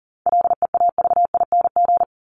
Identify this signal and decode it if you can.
Morse code, heard as a keyed tone, about 40 wpm, 700 Hz. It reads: LEAVING
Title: ING Suffix - Top 100 Words with Base Words 40wpm